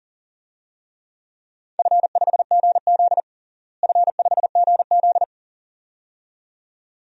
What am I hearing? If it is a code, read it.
Morse code, heard as a keyed tone, about 40 wpm, 700 Hz. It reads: F5GZ F5GZ
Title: DXCC-100 Call Signs - 4 Characters - Repeated 2x - 40wpm